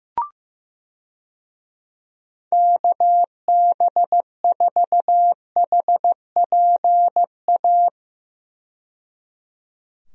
Morse code, heard as a keyed tone, about 15 wpm, 700 Hz. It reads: KB4HPA
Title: USA Call Signs - 6 Characters - 15wpm